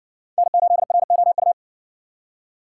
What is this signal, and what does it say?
Morse code, heard as a keyed tone, about 54 wpm, 700 Hz. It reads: N8ROU